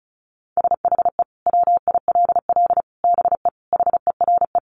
Morse code, heard as a keyed tone, about 34 wpm, 700 Hz. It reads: SHE WILL BE HERE
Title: Sentences from Top 600 Words 35wpm